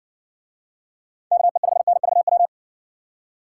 Morse code, heard as a keyed tone, about 60 wpm, 700 Hz. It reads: KE5NVW